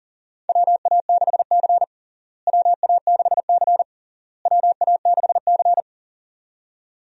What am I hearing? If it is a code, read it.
Morse code, heard as a keyed tone, about 40 wpm, 700 Hz. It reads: WA6C WA6C WA6C
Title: USA Call Signs - 4 Characters - Repeated 3x - 40wpm